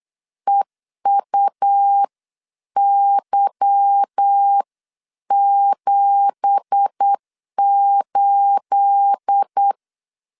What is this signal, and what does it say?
Morse code, heard as a keyed tone, about 8 wpm, 800 Hz. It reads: EUY78
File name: LEZIONI / 10 SIGNS / 10A SLASH 8wpm 3452